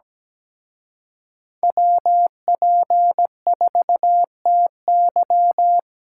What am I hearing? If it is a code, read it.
Morse code, heard as a keyed tone, about 20 wpm, 700 Hz. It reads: WP4TY
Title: Speed Racing - USA 2x2 Call Signs 17wpm